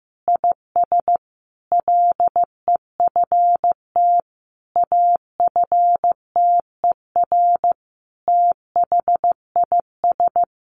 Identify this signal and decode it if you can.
Morse code, heard as a keyed tone, about 14 wpm, 700 Hz. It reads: IS LEFT AFTER THIS
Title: Sentences from Top 1500 Words 15wpm